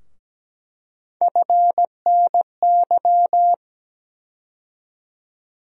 Morse code, heard as a keyed tone, about 17 wpm, 700 Hz. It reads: FNY
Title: USA Call Sign Postfixes - 3 Letters 17wpm